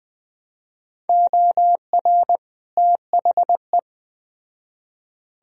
Morse code, heard as a keyed tone, about 21 wpm, 700 Hz. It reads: OR THE